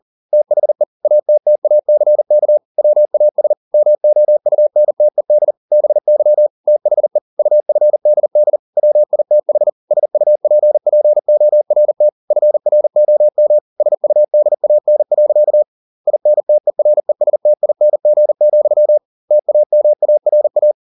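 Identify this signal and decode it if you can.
Morse code, heard as a keyed tone, about 40 wpm, 600 Hz. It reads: THE ATTACK WAS MOUNTED BY THE UFDD WITH SUPPORT FROM SUDAN. INTERESTING, TAMARA
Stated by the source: Morse code at 40 wpm, practice text, CW med Gnister.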